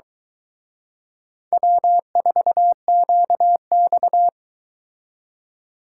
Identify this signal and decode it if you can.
Morse code, heard as a keyed tone, about 22 wpm, 700 Hz. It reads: W4QX